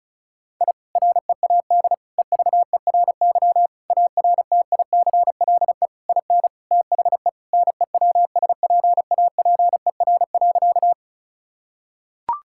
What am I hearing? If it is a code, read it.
Morse code, heard as a keyed tone, about 36 wpm, 700 Hz. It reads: I READ EVERY ARTICLE IN THE NEWSPAPER.
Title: Sentences from Top 1000 Words - Review Part 5 - 35wpm